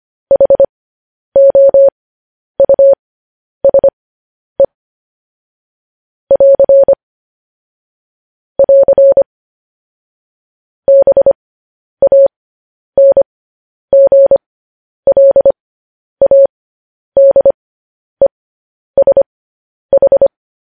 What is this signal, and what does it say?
Morse code, heard as a keyed tone, about 25 wpm, spaced as for 10 wpm, 550 Hz. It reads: HOUSE <AR> <AR> BANGLADESH
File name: podcast / News headlines-1x-25x10